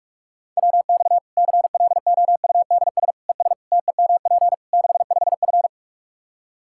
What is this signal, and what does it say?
Morse code, heard as a keyed tone, about 46 wpm, 700 Hz. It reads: WX CLOUDS ES TEMP 65F
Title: Practice QSO - Exchange 2 - Info - 45wpm